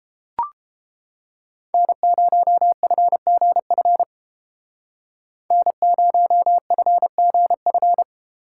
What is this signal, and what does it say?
Morse code, heard as a keyed tone, about 31 wpm, 700 Hz. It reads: N0FGF N0FGF